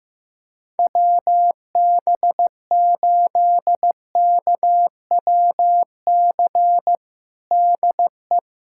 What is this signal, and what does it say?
Morse code, heard as a keyed tone, about 15 wpm, 700 Hz. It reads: WB8KWC DE
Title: USA Callsign DE Callsign - 6 Characters - 15wpm